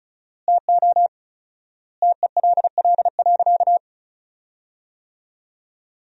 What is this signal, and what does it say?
Morse code, heard as a keyed tone, about 36 wpm, 700 Hz. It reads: TO TELL.
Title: 4x Word-Spacing - Sentences from Top 200 Words - 35wpm